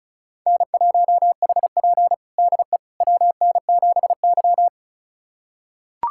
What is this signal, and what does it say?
Morse code, heard as a keyed tone, about 36 wpm, 700 Hz. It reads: N1HP DE WN7Y
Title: USA Callsign DE Callsign - 4 Characters - 35wpm